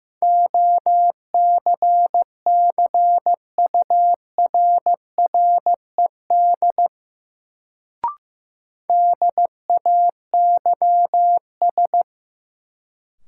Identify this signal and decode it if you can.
Morse code, heard as a keyed tone, about 15 wpm, 700 Hz. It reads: OCCURRED DAYS
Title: Sets of 5 Words Encore - New Word Drill - 15wpm